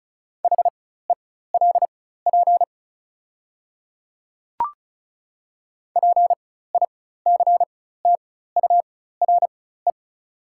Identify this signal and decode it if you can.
Morse code, heard as a keyed tone, about 34 wpm, spaced as for 16 wpm, 700 Hz. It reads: HELP PICTURE
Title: 4x Character Spacing - Top 200 Words - 35wpm